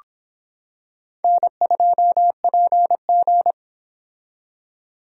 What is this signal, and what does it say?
Morse code, heard as a keyed tone, about 25 wpm, 700 Hz. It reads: N2PG